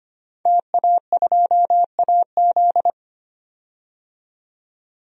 Morse code, heard as a keyed tone, about 25 wpm, 700 Hz. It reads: TA2AZ
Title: DXCC-100 Call Signs - 5 Characters - 25wpm